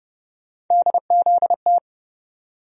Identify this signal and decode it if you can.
Morse code, heard as a keyed tone, about 30 wpm, 700 Hz. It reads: DZT
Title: USA Call Sign Postfixes - 3 Letters 30wpm